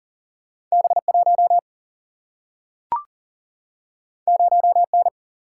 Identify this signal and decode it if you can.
Morse code, heard as a keyed tone, about 40 wpm, 700 Hz. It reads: B1 0N
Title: Single-Letter Single-Number 40wpm